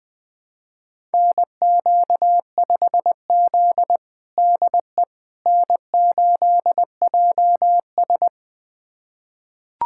Morse code, heard as a keyed tone, about 21 wpm, 700 Hz. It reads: NQ5Z DE N8JS
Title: USA Callsign DE Callsign - 4 Characters - 20wpm